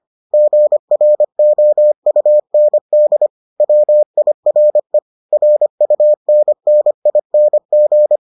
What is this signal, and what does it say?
Morse code, heard as a keyed tone, about 25 wpm, 600 Hz. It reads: GROUND WIRE RUNNING